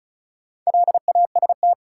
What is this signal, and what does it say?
Morse code, heard as a keyed tone, about 36 wpm, 700 Hz. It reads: LAST
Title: Sentences from Top 300 Words - New Word Drill - 35wpm